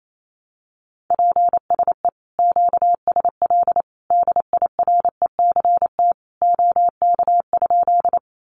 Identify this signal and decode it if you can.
Morse code, heard as a keyed tone, about 28 wpm, 700 Hz. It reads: PSE QSL DIRECT OK?
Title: Common QSO Phrases 28wpm